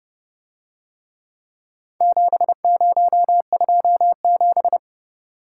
Morse code, heard as a keyed tone, about 29 wpm, 700 Hz. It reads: 7027